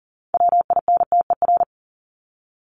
Morse code, heard as a keyed tone, about 40 wpm, 700 Hz. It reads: WINTER
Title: Top 1501-1750 Words 40wpm